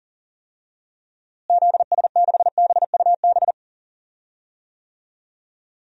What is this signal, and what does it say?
Morse code, heard as a keyed tone, about 40 wpm, 700 Hz. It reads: ZS6BUB
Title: DXCC-100 Call Signs - 6 Characters - Repeated 2x - 40wpm